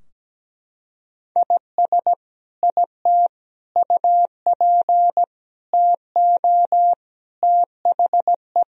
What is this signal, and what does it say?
Morse code, heard as a keyed tone, about 17 wpm, 700 Hz. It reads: IS IT UP TO THE
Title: Sentences from Top 1000 Words - Review Part 4 - 17wpm